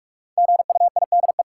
Morse code, heard as a keyed tone, about 46 wpm, 700 Hz. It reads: GUIDE
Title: Sentences from Top 600 Words - New Word Drill - 45wpm